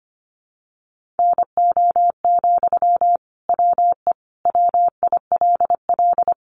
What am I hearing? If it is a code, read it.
Morse code, heard as a keyed tone, about 25 wpm, 700 Hz. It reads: NO, WE WILL
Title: Sentences from Top 1100 Words 25wpm